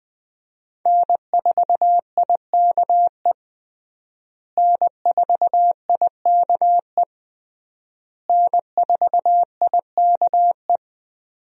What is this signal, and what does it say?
Morse code, heard as a keyed tone, about 21 wpm, 700 Hz. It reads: N4IKE N4IKE N4IKE
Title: USA Call Signs - 5 Characters - Repeated 3x - 20wpm